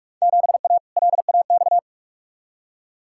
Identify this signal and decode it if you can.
Morse code, heard as a keyed tone, about 46 wpm, 700 Hz. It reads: 7A LAX
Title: Field Day Exchanges - 45wpm